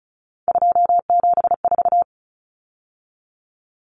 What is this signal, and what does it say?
Morse code, heard as a keyed tone, about 34 wpm, 700 Hz. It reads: 274
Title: Numbers - 3-Digit 35wpm